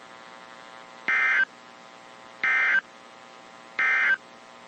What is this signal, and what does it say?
Emergency Alert System (SAME) tones, heard as harsh buzzing bursts. It is heard from 1.0 to 4.3 s.
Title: WZ2523 - NOAA Weather Radio